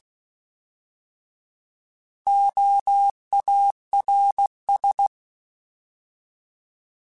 Morse code, heard as a keyed tone, about 16 wpm, 800 Hz. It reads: OARS
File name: Alphabetical / Alphabetical mp3s / Os @15